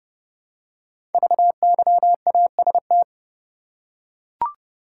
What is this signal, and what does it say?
Morse code, heard as a keyed tone, about 30 wpm, 700 Hz. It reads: VYAST
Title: Code Groups - 5 Letters - 30wpm